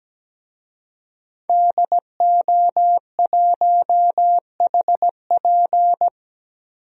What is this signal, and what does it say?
Morse code, heard as a keyed tone, about 17 wpm, 700 Hz. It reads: DO1HP
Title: DXCC-100 Call Signs - 5 Characters - Repeated 2x - 17wpm